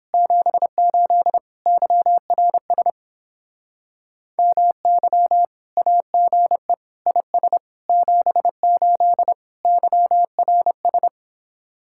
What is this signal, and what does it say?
Morse code, heard as a keyed tone, about 26 wpm, 700 Hz. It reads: 78 YRS MY AGE IS 78 YRS
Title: Speed-Racing Common QSO Phrases - 22wpm